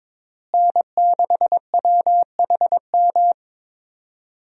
Morse code, heard as a keyed tone, about 22 wpm, 700 Hz. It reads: N6WHM
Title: USA Call Signs - 5 Characters - Repeated 2x - 22wpm